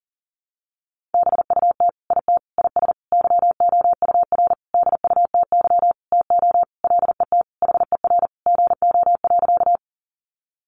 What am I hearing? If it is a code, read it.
Morse code, heard as a keyed tone, about 40 wpm, 700 Hz. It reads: BUT IT IS YOUR DUTY TO LET HER GO.
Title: Sentences from Top 1500 Words 40wpm